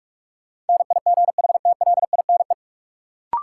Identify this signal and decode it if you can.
Morse code, heard as a keyed tone, about 46 wpm, 700 Hz. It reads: NIGHTLINE
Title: Sets of 5 Words - New Word Drill - 45wpm